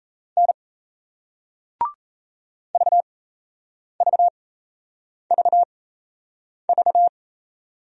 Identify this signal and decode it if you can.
Morse code, heard as a keyed tone, about 35 wpm, spaced as for 8 wpm, 700 Hz. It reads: N VVVV
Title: Speed-Racing Single Letters 28wpm